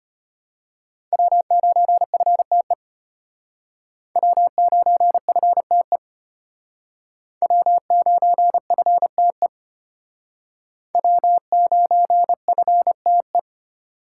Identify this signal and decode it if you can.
Morse code, heard as a keyed tone, about 33 wpm, 700 Hz. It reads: W9FTE W9FTE W9FTE W9FTE